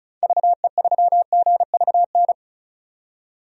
Morse code, heard as a keyed tone, about 36 wpm, 700 Hz. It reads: VE3GVN
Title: DXCC-100 Call Signs - 6 Characters - 35wpm